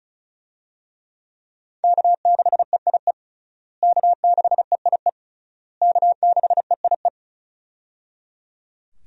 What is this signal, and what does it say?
Morse code, heard as a keyed tone, about 36 wpm, 700 Hz. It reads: K6EIE K6EIE K6EIE
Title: USA Call Signs - 5 Characters - Repeated 3x - 35wpm